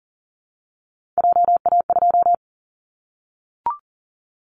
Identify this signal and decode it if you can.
Morse code, heard as a keyed tone, about 40 wpm, 700 Hz. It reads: JA2
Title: Callsign Prefixes for Top 25 DXCC - Repeated 3x - 40wpm